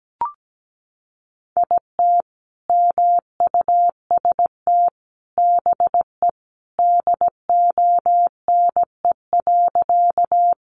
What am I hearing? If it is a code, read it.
Morse code, heard as a keyed tone, about 17 wpm, 700 Hz. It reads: IT MUST BE DONE.
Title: Sentences from Top 500 Words 17wpm